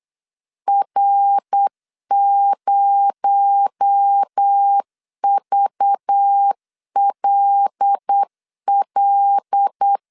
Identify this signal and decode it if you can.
Morse code, heard as a keyed tone, about 8 wpm, 800 Hz. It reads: R0VLL
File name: CW / DET / 1 6 1500C S B 8wpm 5142